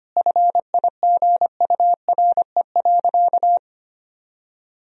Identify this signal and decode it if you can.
Morse code, heard as a keyed tone, about 25 wpm, 700 Hz. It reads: FIGURE.
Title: Sentences from Top 1000 Words - Review Part 4 - 25wpm